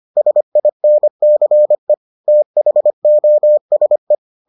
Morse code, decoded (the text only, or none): SINCE THOSE